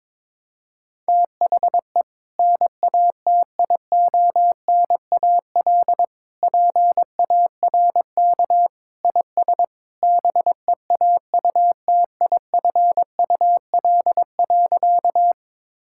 Morse code, heard as a keyed tone, about 22 wpm, 700 Hz. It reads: THE NATIONAL PARK IS BEAUTIFUL.
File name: practice / 22wpm / Sentences with 2-Word Phrases - 22wpm